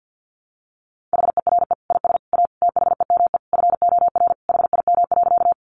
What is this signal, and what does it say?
Morse code, heard as a keyed tone, about 50 wpm, 700 Hz. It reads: HERE IS A THEME FOR HIM.
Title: Sentences from Top 1100 Words 50wpm